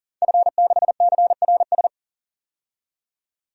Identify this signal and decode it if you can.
Morse code, heard as a keyed tone, about 40 wpm, 700 Hz. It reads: F6CRS
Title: DXCC-100 Call Signs - 5 Characters - 40wpm